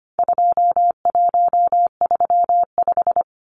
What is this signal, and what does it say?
Morse code, heard as a keyed tone, about 25 wpm, 700 Hz. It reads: 2135
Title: Numbers - 4-Digit 25wpm